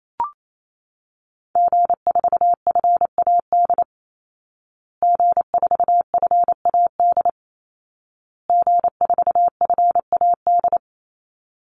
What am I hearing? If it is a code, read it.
Morse code, heard as a keyed tone, about 28 wpm, 700 Hz. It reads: G4FAD G4FAD G4FAD
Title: Call Signs Repeated 2x - 28wpm